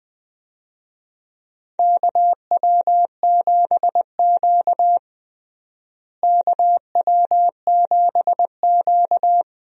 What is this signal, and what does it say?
Morse code, heard as a keyed tone, about 20 wpm, 700 Hz. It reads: KW7Q KW7Q
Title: USA Call Signs - 4 Characters - Repeated 2x - 20wpm